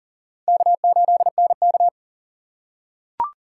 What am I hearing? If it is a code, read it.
Morse code, heard as a keyed tone, about 37 wpm, 700 Hz. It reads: K8NK
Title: USA Call Signs - 4x Character Spacing - 40wpm